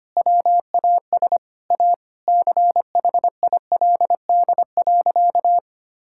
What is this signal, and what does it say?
Morse code, heard as a keyed tone, about 25 wpm, 700 Hz. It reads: WAS A CHILD.
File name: practice / 25wpm / Sentences with 4-Word Phrases - 25wpm